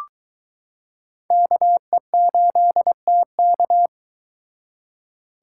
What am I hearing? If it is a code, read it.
Morse code, heard as a keyed tone, about 22 wpm, 700 Hz. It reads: KE8TK